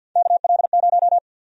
Morse code, heard as a keyed tone, about 52 wpm, 700 Hz. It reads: KL0
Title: USA Call Sign Prefixes - 50wpm